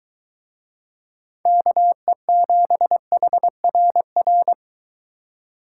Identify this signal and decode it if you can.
Morse code, heard as a keyed tone, about 23 wpm, 700 Hz. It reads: KE7HRR